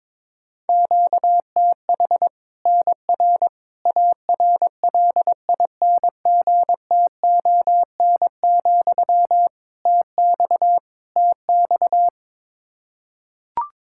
Morse code, heard as a keyed tone, about 22 wpm, 700 Hz. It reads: QTH NR ARLINGTON, TX TX
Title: Practice QSO - Exchange 1 - Beginning - 22wpm